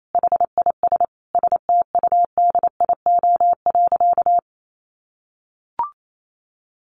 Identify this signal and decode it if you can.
Morse code, heard as a keyed tone, about 28 wpm, 700 Hz. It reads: HIS STUDIO.
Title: Sentences from Top 1200 Words 28wpm